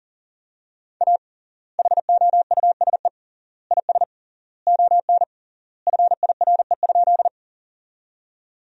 Morse code, heard as a keyed tone, about 40 wpm, 700 Hz. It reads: A HOUSE IS ON FIRE?
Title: 3x Word-Spacing - Sentences from Top 400 Words - 40wpm